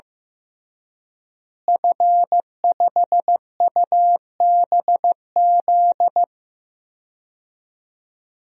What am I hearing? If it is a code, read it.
Morse code, heard as a keyed tone, about 16 wpm, 700 Hz. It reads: F5UBZ
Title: DXCC-100 Call Signs - 5 Characters - Repeated 3x - 15wpm